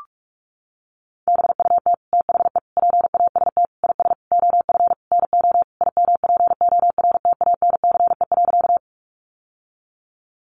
Morse code, heard as a keyed tone, about 44 wpm, 700 Hz. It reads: BUT THE PAST IS OF NO IMPORTANCE.
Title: Sentences from Top 1200 Words 45wpm